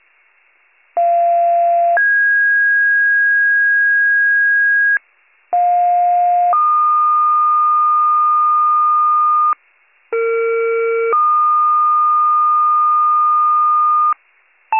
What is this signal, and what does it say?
Two-tone page, heard as a one-second tone, then a three-second tone, about 690 then 1650 Hz, about 710 then 1220 Hz, about 460 then 1220 Hz. It is heard from 1.0 to 14.1 s.